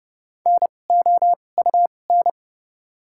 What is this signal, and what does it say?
Morse code, heard as a keyed tone, about 30 wpm, spaced as for 22 wpm, 700 Hz. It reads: NOUN
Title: 2x Character Spacing - Top 500 Words - 30wpm